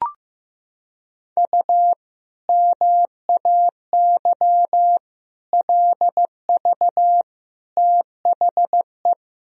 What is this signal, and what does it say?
Morse code, heard as a keyed tone, about 16 wpm, 700 Hz. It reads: U MAY LV THE
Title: Sentences from Top 400 Words and Abbreviations 15wpm